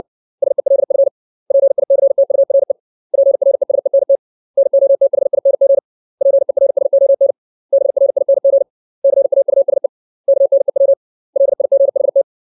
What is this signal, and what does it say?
Morse code, heard as a keyed tone, about 60 wpm, 550 Hz. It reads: SELF PICTURE PRESENT NOTHING PERSON BRING CAUSE KNEW LIGHT
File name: 632 words / 632 Words 60WPM